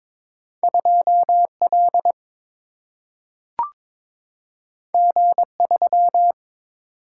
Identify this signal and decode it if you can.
Morse code, heard as a keyed tone, about 22 wpm, 700 Hz. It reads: 2L G3
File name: practice / 22wpm / Two Letter-Number Combinations - 22wpm